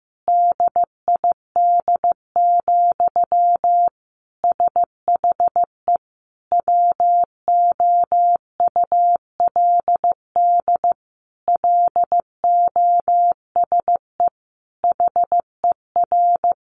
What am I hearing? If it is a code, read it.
Morse code, heard as a keyed tone, about 15 wpm, 700 Hz. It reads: DID, SHE WOULD LOSE HER
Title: Sentences from Top 1400 Words 15wpm